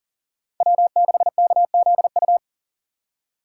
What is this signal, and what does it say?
Morse code, heard as a keyed tone, about 40 wpm, 700 Hz. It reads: W6KZU
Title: USA Call Signs - 5 Characters - 3x Character Spacing - 40wpm